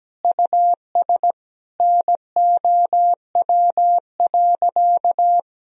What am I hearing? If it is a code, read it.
Morse code, heard as a keyed tone, about 17 wpm, 700 Hz. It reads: US NOW.
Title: Sentences from Top 1000 Words - Review Part 3 - 17wpm